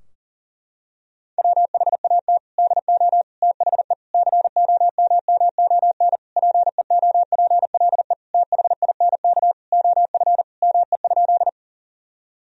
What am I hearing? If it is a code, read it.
Morse code, heard as a keyed tone, about 37 wpm, 700 Hz. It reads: WHAT DO THE COMMON PEOPLE THINK OF ME?
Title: Sentences from Top 1000 Words - Review Part 3 - 40wpm